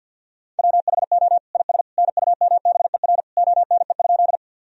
Morse code, heard as a keyed tone, about 50 wpm, 700 Hz. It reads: WHO IS NUMBER ONE?
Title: Sentences from Top 1000 Words - Review Part 1 - 50wpm